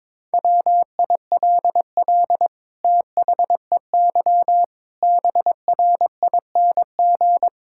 Morse code, heard as a keyed tone, about 22 wpm, 700 Hz. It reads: WILL THEY BRING
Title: Sentences from Top 1000 Words - Review Part 6 - 22wpm